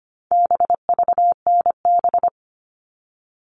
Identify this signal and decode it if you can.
Morse code, heard as a keyed tone, about 25 wpm, 700 Hz. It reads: BVNB
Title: All Cut Numbers - 4-Digit 25wpm